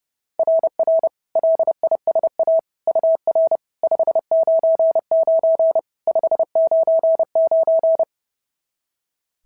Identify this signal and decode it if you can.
Morse code, heard as a keyed tone, about 26 wpm, 650 Hz. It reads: RR LISA UR 599 599
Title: SOTA QSO Practice 30wpm